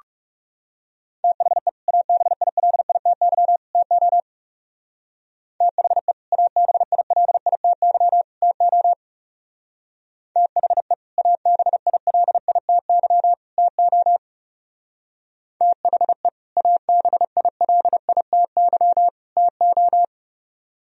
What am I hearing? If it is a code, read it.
Morse code, heard as a keyed tone, about 37 wpm, 700 Hz. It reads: THE ABILITY TO THE ABILITY TO THE ABILITY TO THE ABILITY TO
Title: Speed-Racing - Sets of 3 Words 30wpm